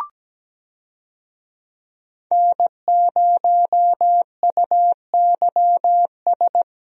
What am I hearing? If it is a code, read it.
Morse code, heard as a keyed tone, about 17 wpm, 700 Hz. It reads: N0UYS
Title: USA Call Signs - 5 Characters - Repeated 3x - 17wpm